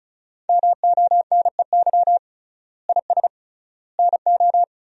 Morse code, heard as a keyed tone, about 36 wpm, 700 Hz. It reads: MONEY IS NO
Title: 3x Word-Spacing - Sentences from Top 500 Words - 35wpm